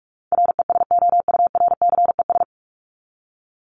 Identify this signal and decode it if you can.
Morse code, heard as a keyed tone, about 44 wpm, 700 Hz. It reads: RESOURCES
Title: ES:S Suffix - Top 100 Words with Base Words 45wpm